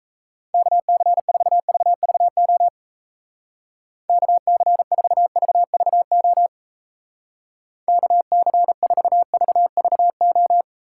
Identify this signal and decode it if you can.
Morse code, heard as a keyed tone, about 37 wpm, 700 Hz. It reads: KC4VVO KC4VVO KC4VVO